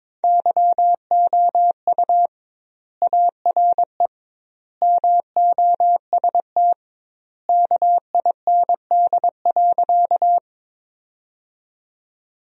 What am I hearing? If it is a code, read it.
Morse code, heard as a keyed tone, about 22 wpm, 700 Hz. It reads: YOU ARE MOST KIND.
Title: 2x Word-Spacing - Sentences from Top 200 Words - 22wpm